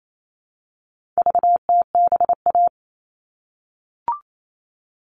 Morse code, heard as a keyed tone, about 28 wpm, 700 Hz. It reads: VTBA
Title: All Cut Numbers - 4-Digit 28wpm